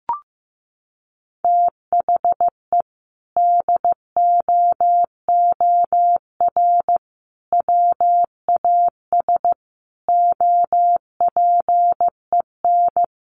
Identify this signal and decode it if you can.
Morse code, heard as a keyed tone, about 15 wpm, 700 Hz. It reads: THE DOOR WAS OPEN
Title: Sentences from Top 1300 Words 15wpm